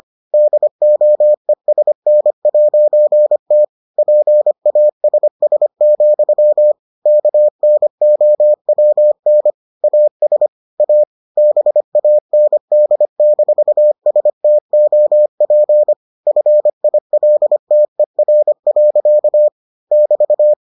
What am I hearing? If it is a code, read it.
Morse code, heard as a keyed tone, about 25 wpm, 600 Hz. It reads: DOESN'T PASS, KNOWN AS A BAND-STOP FILTER. <BT>